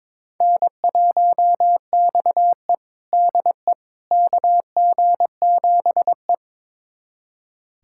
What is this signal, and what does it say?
Morse code, heard as a keyed tone, about 22 wpm, 700 Hz. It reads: N1XE DE KG7E
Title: USA Callsign DE Callsign - 4 Characters - 22wpm